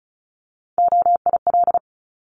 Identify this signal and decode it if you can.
Morse code, heard as a keyed tone, about 36 wpm, 700 Hz. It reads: OIL
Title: Multi-letter Phonemes 35wpm